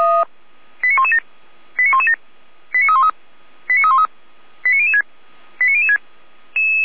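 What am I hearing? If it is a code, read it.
Five-tone selective call (ZVEI), heard as quick runs of five tones, tones of about 70 ms. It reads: 89117 89117 89313 89313 89006 89006
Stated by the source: Five-tone ZVEI selective call received by SDR, from the audio archive of feuerwehr-orlamuende.org (fire brigade).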